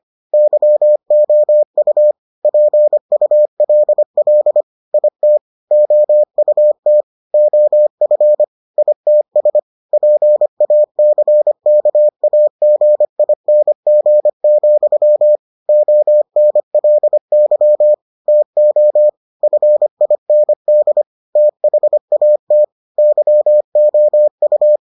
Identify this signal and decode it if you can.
Morse code, heard as a keyed tone, about 25 wpm, 600 Hz. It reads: YOU PULL IT OUT OF ITS PACKAGING, ONLY TO FIND THAT YOU